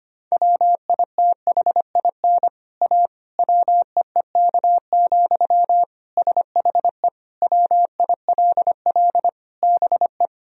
Morse code, heard as a keyed tone, about 25 wpm, 700 Hz. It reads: WITHIN A WEEK, SHE WILL BE
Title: Sentences from Top 1000 Words - Review Part 4 - 25wpm